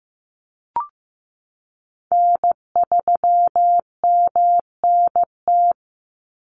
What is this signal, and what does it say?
Morse code, heard as a keyed tone, about 15 wpm, 700 Hz. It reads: N3MNT
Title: Call Signs Repeated 3x - 15wpm